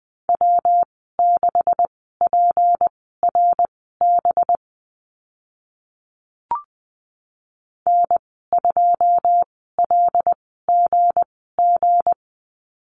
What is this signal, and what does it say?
Morse code, heard as a keyed tone, about 20 wpm, spaced as for 15 wpm, 700 Hz. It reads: W6PRB N2LGG
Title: 193 Lesson 31g - Letter v - Call Signs 20wpm